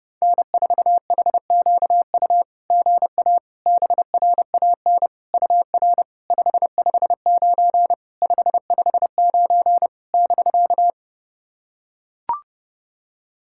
Morse code, heard as a keyed tone, about 30 wpm, 690 Hz. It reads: N4HQU GA BRAN UR 559 559 <BK>
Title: SOTA QSO Practice 30wpm